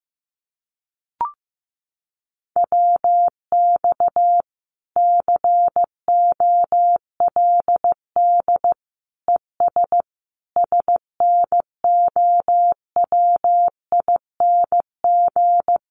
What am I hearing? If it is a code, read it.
Morse code, heard as a keyed tone, about 15 wpm, 700 Hz. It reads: WX COLD ES SNOWING
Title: Common QSO Phrases 15wpm